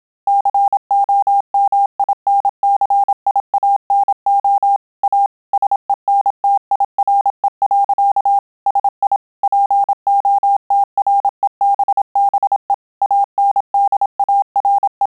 Morse code, heard as a keyed tone, about 27 wpm, 800 Hz. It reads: COMINCIANO A SENTIRE. SI POTREBBE ANDARE